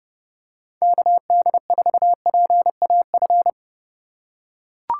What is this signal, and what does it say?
Morse code, heard as a keyed tone, about 30 wpm, 700 Hz. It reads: KD4PAF